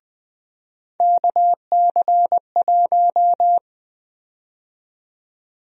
Morse code, heard as a keyed tone, about 20 wpm, 700 Hz. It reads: KC1
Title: USA Call Sign Prefixes - 20wpm